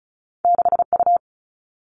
Morse code, heard as a keyed tone, about 36 wpm, 700 Hz. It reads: 6 U
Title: All Cut Numbers - 2-Digit 35wpm